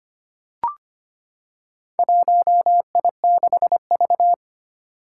Morse code, heard as a keyed tone, about 25 wpm, 700 Hz. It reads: 1I6V
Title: Four Challenging Letter-Number Combinations 25wpm